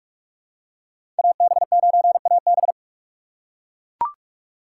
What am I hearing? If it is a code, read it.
Morse code, heard as a keyed tone, about 41 wpm, 700 Hz. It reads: AB9AB
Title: USA Call Signs - 5 Characters - 4x Character Spacing - 45wpm